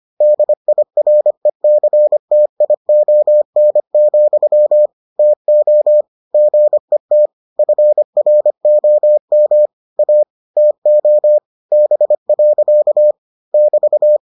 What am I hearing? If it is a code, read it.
Morse code, heard as a keyed tone, about 25 wpm, 600 Hz. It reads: DIRECTION, TO GET FROM A TO B. <BT>